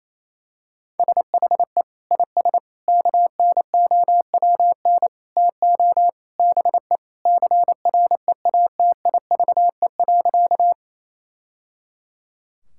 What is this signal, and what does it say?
Morse code, heard as a keyed tone, about 28 wpm, 700 Hz. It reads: SHE IS KNOWN TO BE CREATIVE.
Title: Sentences from Top 1000 Words - Review Part 5 - 28wpm